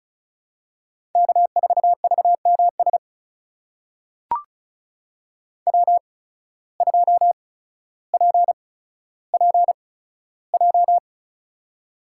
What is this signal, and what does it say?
Morse code, heard as a keyed tone, about 36 wpm, spaced as for 15 wpm, 700 Hz. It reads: K4VMS W2PPJ
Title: USA Call Signs - 5 Characters - 8x Character Spacing - 35wpm